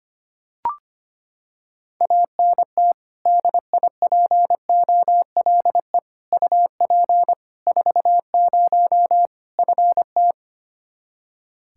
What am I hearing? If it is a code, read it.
Morse code, heard as a keyed tone, about 25 wpm, 700 Hz. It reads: ANT DIPOLE UP 40 FT
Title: Practice QSO - Exchange 2 - Info - 25wpm